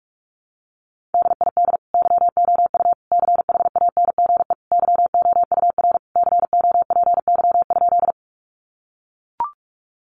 Morse code, heard as a keyed tone, about 44 wpm, 700 Hz. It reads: DID YOU CHANGE YOUR COPY?
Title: Sentences from Top 1400 Words 45wpm